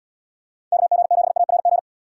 Morse code, heard as a keyed tone, about 73 wpm, 700 Hz. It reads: KG7ARG